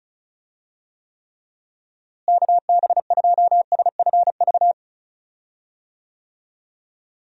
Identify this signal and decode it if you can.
Morse code, heard as a keyed tone, about 36 wpm, 700 Hz. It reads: KB2SFV
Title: USA Call Signs - 6 Characters - Repeated 2x - 35wpm